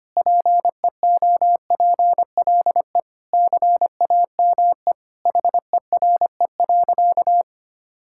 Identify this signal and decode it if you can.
Morse code, heard as a keyed tone, about 25 wpm, 700 Hz. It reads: PEOPLE CAME HERE.
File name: practice / 25wpm / Sentences with 2-Word Phrases - 25wpm